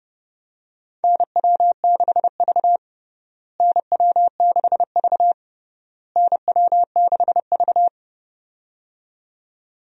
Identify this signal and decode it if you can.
Morse code, heard as a keyed tone, about 30 wpm, 700 Hz. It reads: NW6V NW6V NW6V
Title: USA Call Signs - 4 Characters - Repeated 3x - 30wpm